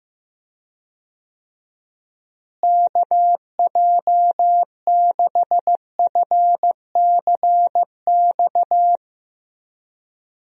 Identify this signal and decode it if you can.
Morse code, heard as a keyed tone, about 15 wpm, 700 Hz. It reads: KJ6FCX
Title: USA Call Signs - 6 Characters - Repeated 3x - 15wpm